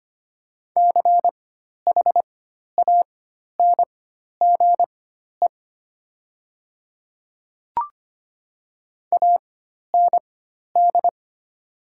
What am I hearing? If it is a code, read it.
Morse code, heard as a keyed tone, about 25 wpm, spaced as for 12 wpm, 700 Hz. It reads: CHANGE AND
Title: 4x Character Spacing - Top 200 Words - 25wpm